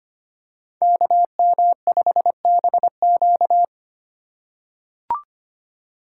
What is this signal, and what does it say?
Morse code, heard as a keyed tone, about 25 wpm, 700 Hz. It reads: KM5BQ
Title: USA Call Signs - 5 Characters - 4x Character Spacing - 25wpm